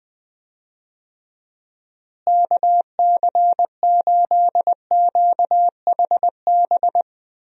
Morse code, heard as a keyed tone, about 20 wpm, 700 Hz. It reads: KC8QHB
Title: USA Call Signs - 6 Characters - Repeated 3x - 20wpm